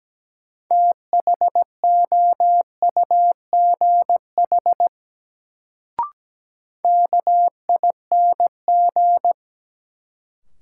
Morse code, heard as a keyed tone, about 17 wpm, 700 Hz. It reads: THOUGH KING
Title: Sentences from Top 400 Words - New Word Drill - 17wpm